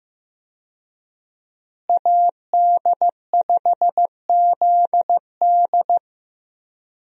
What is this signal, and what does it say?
Morse code, heard as a keyed tone, about 15 wpm, 700 Hz. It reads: AD5ZD
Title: USA Call Signs - 5 Characters - Repeated 3x - 15wpm